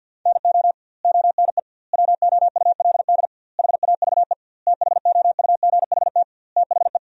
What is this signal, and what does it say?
Morse code, heard as a keyed tone, about 49 wpm, 700 Hz. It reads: NO ONE WOULD HAVE THOUGHT THE
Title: 2x Word-Spacing - Sentences from Top 300 Words - 50wpm